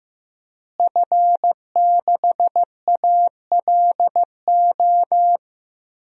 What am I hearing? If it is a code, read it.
Morse code, heard as a keyed tone, about 15 wpm, 700 Hz. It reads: F6ALO
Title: DXCC-100 Call Signs - 5 Characters - Repeated 3x - 15wpm